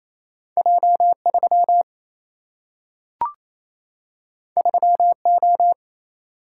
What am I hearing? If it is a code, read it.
Morse code, heard as a keyed tone, about 28 wpm, 700 Hz. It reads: J 3 3 O
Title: Single-Letter Single-Number 28wpm